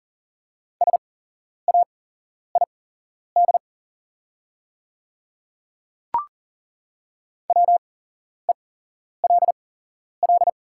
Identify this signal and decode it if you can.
Morse code, heard as a keyed tone, about 40 wpm, spaced as for 11 wpm, 690 Hz. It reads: SAID WELL
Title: 8x Character Spacing - Top 200 Words - 40wpm